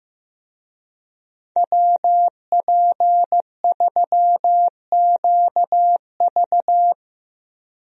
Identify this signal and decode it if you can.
Morse code, heard as a keyed tone, about 15 wpm, 700 Hz. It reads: WP3QV
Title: USA Call Signs - 5 Characters - Repeated 3x - 15wpm